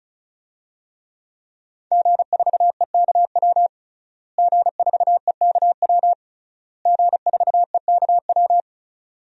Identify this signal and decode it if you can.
Morse code, heard as a keyed tone, about 36 wpm, 700 Hz. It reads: G4EKW G4EKW G4EKW
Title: DXCC-100 Call Signs - 5 Characters - Repeated 3x - 35wpm